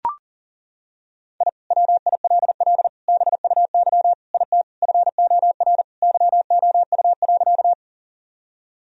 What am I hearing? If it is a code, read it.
Morse code, heard as a keyed tone, about 37 wpm, 690 Hz. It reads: I WILL BUY IT FOR YOU.
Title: Sentences from Top 1000 Words - Review Part 4 - 40wpm